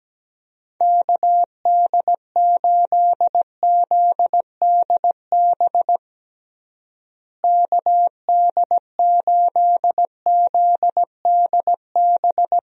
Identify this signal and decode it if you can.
Morse code, heard as a keyed tone, about 17 wpm, 700 Hz. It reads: KD8ZDB KD8ZDB
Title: USA Call Signs - 6 Characters - Repeated 3x - 17wpm